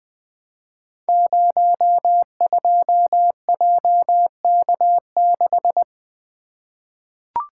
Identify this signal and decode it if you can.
Morse code, heard as a keyed tone, about 20 wpm, 700 Hz. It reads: 02JK6
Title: Code Groups - 5 Letter:Numbers - 20wpm